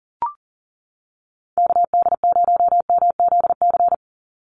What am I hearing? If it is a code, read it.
Morse code, heard as a keyed tone, about 38 wpm, 700 Hz. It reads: KD0MZC
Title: USA Call Signs 40wpm